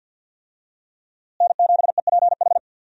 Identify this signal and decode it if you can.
Morse code, heard as a keyed tone, about 46 wpm, 700 Hz. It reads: N7EPH
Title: USA Call Signs - 5 Characters - Repeated 2x - 50wpm